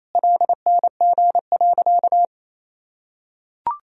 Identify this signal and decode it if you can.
Morse code, heard as a keyed tone, about 28 wpm, 700 Hz. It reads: LNG.
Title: Sentences from Top 500 Words and Abbreviations 28wpm